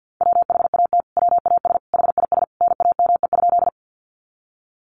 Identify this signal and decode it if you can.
Morse code, heard as a keyed tone, about 50 wpm, 700 Hz. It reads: WHAT WAS HIS NAME?
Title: Sentences from Top 200 Words 50wpm